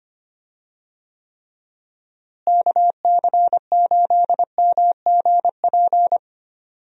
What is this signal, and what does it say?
Morse code, heard as a keyed tone, about 25 wpm, 700 Hz. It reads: KC8MGP
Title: USA Call Signs - 6 Characters - Repeated 2x - 25wpm